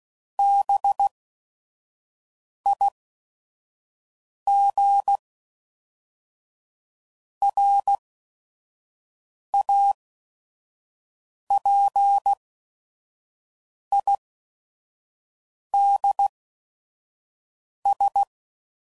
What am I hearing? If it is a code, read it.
Morse code, heard as a keyed tone, about 16 wpm, spaced as for 5 wpm, 800 Hz. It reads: BIGRAPIDS
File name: CW Code Groups @5WPM / fromSM's / Cities